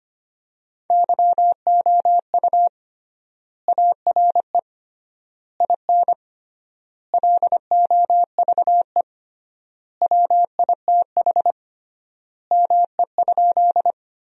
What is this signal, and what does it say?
Morse code, heard as a keyed tone, about 25 wpm, 700 Hz. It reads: YOU ARE IN LOVE WITH ME?
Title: 3x Word-Spacing - Sentences from Top 400 Words - 25wpm